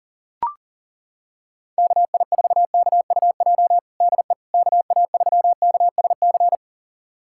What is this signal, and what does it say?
Morse code, heard as a keyed tone, about 40 wpm, 700 Hz. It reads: KI4KUJ DE KA3KSC
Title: USA Callsign DE Callsign 40wpm